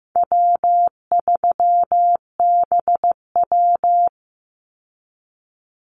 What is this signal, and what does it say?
Morse code, heard as a keyed tone, about 15 wpm, 700 Hz. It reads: W3BW
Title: Easy Call Signs Repeated 3x - 15wpm